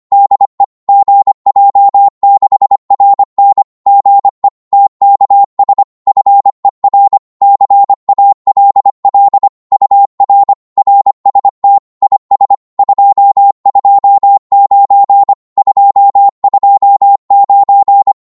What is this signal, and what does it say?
Morse code, heard as a keyed tone, about 25 wpm, 800 Hz. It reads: DE GJ6RN GE TKS FER CALL UR RST IS 229 229
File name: Morse / qso-25wpm